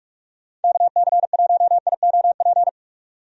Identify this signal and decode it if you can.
Morse code, heard as a keyed tone, about 46 wpm, 700 Hz. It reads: KC1IOP
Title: USA Call Signs - 6 Characters - 45wpm